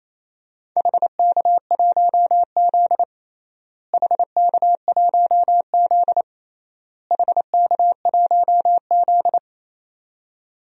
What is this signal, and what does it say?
Morse code, heard as a keyed tone, about 28 wpm, 700 Hz. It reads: HK1Z HK1Z HK1Z
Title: DXCC-100 Call Signs - 4 Characters - Repeated 3x - 28wpm